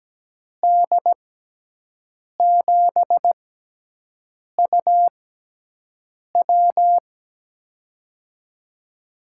Morse code, heard as a keyed tone, about 17 wpm, spaced as for 6 wpm, 700 Hz. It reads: D7UW
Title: USA Call Signs - 5 Characters - 6x Character Spacing - 17wpm